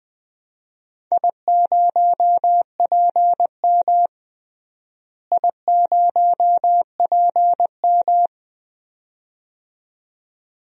Morse code, heard as a keyed tone, about 20 wpm, 700 Hz. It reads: I0PM I0PM
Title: DXCC-100 Call Signs - 4 Characters - Repeated 2x - 20wpm